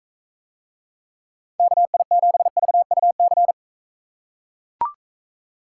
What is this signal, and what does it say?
Morse code, heard as a keyed tone, about 42 wpm, 700 Hz. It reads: KI7VUC